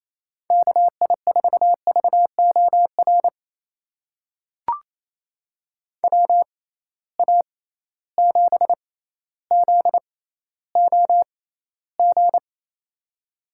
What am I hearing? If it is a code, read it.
Morse code, heard as a keyed tone, about 28 wpm, spaced as for 14 wpm, 700 Hz. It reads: KI4VOR WA7ZOG
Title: USA Call Signs - 6 Characters - 6x Character Spacing - 28wpm